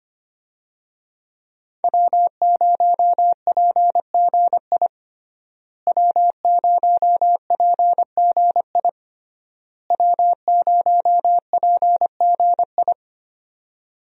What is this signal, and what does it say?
Morse code, heard as a keyed tone, about 25 wpm, 700 Hz. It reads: W0PGI W0PGI W0PGI
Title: USA Call Signs - 5 Characters - Repeated 3x - 25wpm